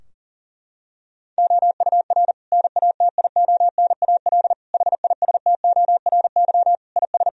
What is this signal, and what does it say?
Morse code, heard as a keyed tone, about 37 wpm, 700 Hz. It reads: OUR NATIONAL HISTORY IS
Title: Sentences from Top 1000 Words - Review Part 4 - 40wpm